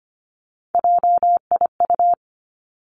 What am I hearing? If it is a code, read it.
Morse code, heard as a keyed tone, about 25 wpm, 700 Hz. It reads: JIU
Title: Three Challenging Letter-Number Combinations 25wpm